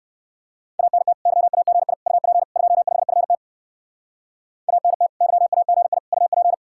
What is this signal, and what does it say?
Morse code, heard as a keyed tone, about 69 wpm, 700 Hz. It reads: ANT YAGI UP 25FT ANT YAGI UP